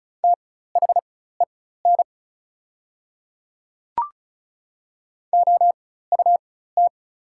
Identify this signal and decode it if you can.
Morse code, heard as a keyed tone, about 36 wpm, spaced as for 16 wpm, 700 Hz. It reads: THEN OUT
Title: 4x Character Spacing - Top 100 Words - 35wpm